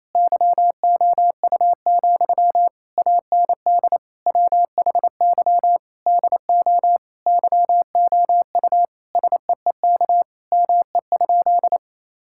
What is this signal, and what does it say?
Morse code, heard as a keyed tone, about 28 wpm, 700 Hz. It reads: YOU, AND WHY DO YOU SEEK ME?
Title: Sentences from Top 1600 Words 28wpm